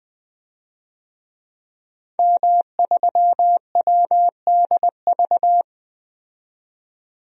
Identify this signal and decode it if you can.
Morse code, heard as a keyed tone, about 20 wpm, 700 Hz. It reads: M3WDV
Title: DXCC-100 Call Signs - 5 Characters - 20wpm